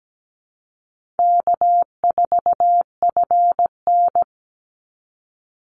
Morse code, heard as a keyed tone, about 17 wpm, 700 Hz. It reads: K4FN
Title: Call Signs Repeated 2x - 17wpm